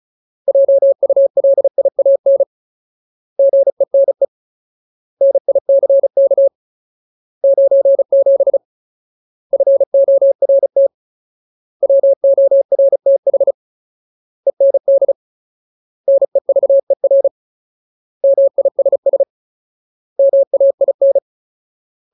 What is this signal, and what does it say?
Morse code, heard as a keyed tone, about 34 wpm, 550 Hz. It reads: JULIAN GENE NICK 97 FORT WORTH END NEVER MISS MAIN
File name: Ultimate Word List / 35WPM Ultimate